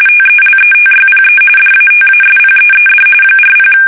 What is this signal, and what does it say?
RTTY, about 45.45 baud. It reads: TELETYPE DATASPEED 40
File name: fsk850 / teletype-corp